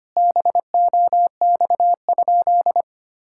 Morse code, heard as a keyed tone, about 25 wpm, 700 Hz. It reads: BOX?
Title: Sentences from Top 1000 Words - Review Part 5 - 25wpm